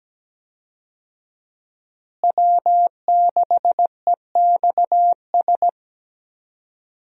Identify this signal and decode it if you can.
Morse code, heard as a keyed tone, about 17 wpm, 700 Hz. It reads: W6EXS
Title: USA Call Signs - 5 Characters - Repeated 3x - 17wpm